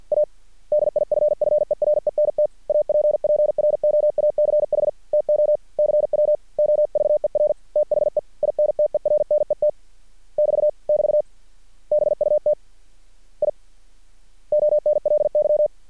Morse code, heard as a keyed tone, about 50 wpm, 600 Hz. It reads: A DIFFERENT APPROACH TO CW OVER THE INTERNET <BT> <BT> BUT I ONLY
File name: archived / CRSnet